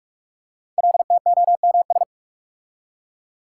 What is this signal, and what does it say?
Morse code, heard as a keyed tone, about 46 wpm, 700 Hz. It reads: LTOMS
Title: Code Groups - 5 Letter:Numbers - 45wpm